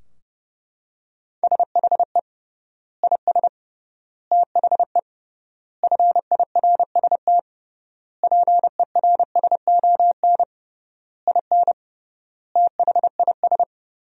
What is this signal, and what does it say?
Morse code, heard as a keyed tone, about 30 wpm, 700 Hz. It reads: SHE IS THE FIRST PERSON IN THIS
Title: 3x Word-Spacing - Sentences from Top 400 Words - 30wpm